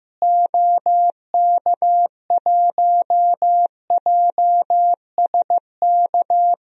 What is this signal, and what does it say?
Morse code, heard as a keyed tone, about 15 wpm, 700 Hz. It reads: OK1JSK
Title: DXCC-100 Call Signs - 6 Characters - Repeated 2x - 15wpm